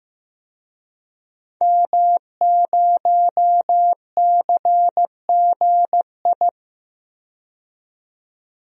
Morse code, heard as a keyed tone, about 15 wpm, 700 Hz. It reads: M0CGI
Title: DXCC-100 Call Signs - 5 Characters - 15wpm